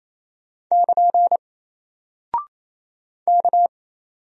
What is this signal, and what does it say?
Morse code, heard as a keyed tone, about 27 wpm, 700 Hz. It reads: <KN> K
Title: Single Character (Letter:Number:Punctuation:Prosign) 28wpm